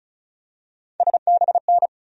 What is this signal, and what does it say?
Morse code, heard as a keyed tone, about 36 wpm, 700 Hz. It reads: SBN
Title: USA Call Sign Postfixes - 3 Letters 35wpm